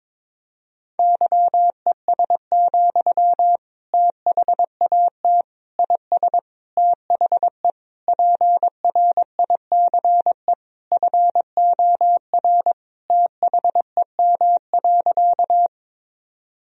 Morse code, heard as a keyed tone, about 21 wpm, 700 Hz. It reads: YES, THAT IS THE PRICE FOR THEM.
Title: Sentences from Top 1000 Words - Review Part 4 - 22wpm